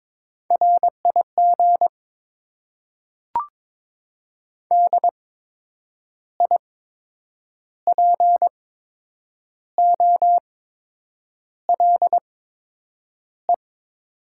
Morse code, heard as a keyed tone, about 22 wpm, spaced as for 8 wpm, 700 Hz. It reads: RIG DIPOLE
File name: Top 90 QSO Elements - 8x Character Spacing - 22wpm